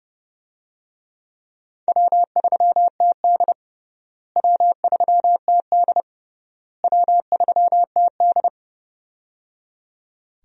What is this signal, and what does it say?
Morse code, heard as a keyed tone, about 30 wpm, 700 Hz. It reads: W3TD W3TD W3TD
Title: USA Call Signs - 4 Characters - Repeated 3x - 30wpm